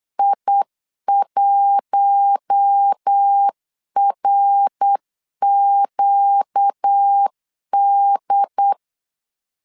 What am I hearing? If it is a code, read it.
Morse code, heard as a keyed tone, about 8 wpm, 800 Hz. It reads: I1RQD
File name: CW / DET / 1 6 1500C S C 8wpm 5219